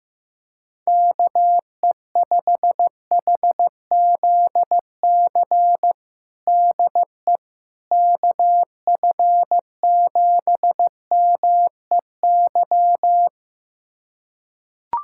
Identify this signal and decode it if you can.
Morse code, heard as a keyed tone, about 16 wpm, 700 Hz. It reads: KE5HZC DE KF7MEY
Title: USA Callsign DE Callsign - 6 Characters - 15wpm